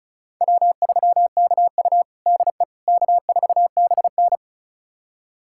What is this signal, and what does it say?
Morse code, heard as a keyed tone, about 36 wpm, 700 Hz. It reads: W3KU DE K4BN
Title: USA Callsign DE Callsign - 4 Characters - 35wpm